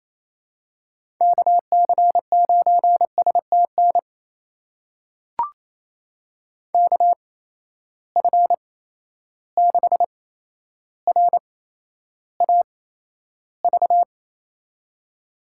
Morse code, heard as a keyed tone, about 28 wpm, spaced as for 12 wpm, 700 Hz. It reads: KC9STN KF6RAV
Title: USA Call Signs - 6 Characters - 8x Character Spacing - 28wpm